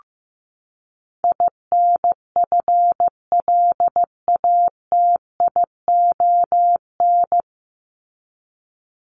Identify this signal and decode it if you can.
Morse code, heard as a keyed tone, about 14 wpm, 700 Hz. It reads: INFLATION
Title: Top 1001-1250 Words 15wpm